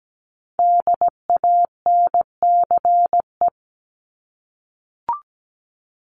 Morse code, heard as a keyed tone, about 17 wpm, 700 Hz. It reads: DANCE
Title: Top 501-750 Words 17wpm